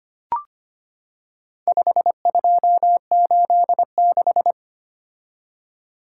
Morse code, heard as a keyed tone, about 25 wpm, 700 Hz. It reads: 5286
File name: practice / 25wpm / Four Letter-Number Combinations - 25wpm